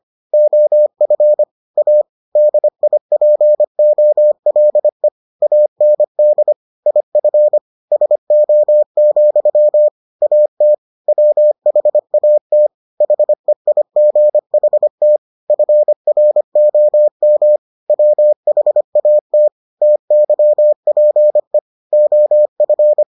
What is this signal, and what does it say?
Morse code, heard as a keyed tone, about 25 wpm, 600 Hz. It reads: OF A DIPOLE AND IF SO, AT WHAT HEIGHT FROM WHAT TYPE OF